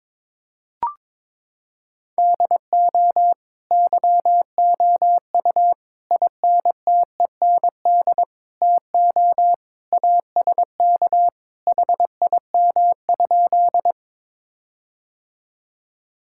Morse code, heard as a keyed tone, about 22 wpm, 700 Hz. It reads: DO YOU INTEND TO ASK HIM?
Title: Sentences from Top 1600 Words 22wpm